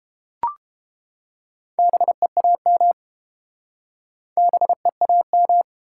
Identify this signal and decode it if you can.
Morse code, heard as a keyed tone, about 30 wpm, 700 Hz. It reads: BEAM BEAM